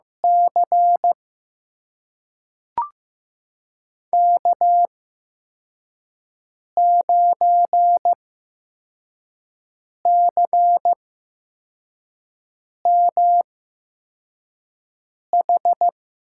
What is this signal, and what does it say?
Morse code, heard as a keyed tone, about 15 wpm, spaced as for 4 wpm, 700 Hz. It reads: C K9CMH
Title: USA Call Signs - 8x Character Spacing - 15wpm